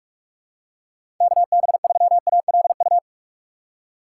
Morse code, heard as a keyed tone, about 48 wpm, 700 Hz. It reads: KB3ALU
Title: USA Call Signs - 6x Character Spacing - 45wpm